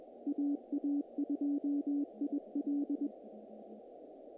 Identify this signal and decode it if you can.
Morse code, heard as a keyed tone, about 23 wpm, 290 Hz. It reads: AA2IL
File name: contests / naqp-cw-k5tr-op-k5pi- -1800